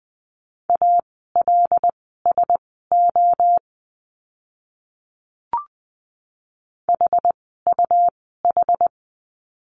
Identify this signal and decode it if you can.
Morse code, heard as a keyed tone, about 20 wpm, spaced as for 14 wpm, 700 Hz. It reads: ALSO HUH
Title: 101 Lesson 17f - Letter u - Words 20wpm